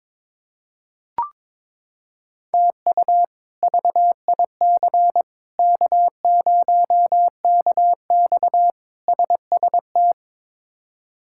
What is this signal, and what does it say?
Morse code, heard as a keyed tone, about 22 wpm, 700 Hz. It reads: TU VIC K0KX SST
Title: K1USN SST QSO Practice - 22wpm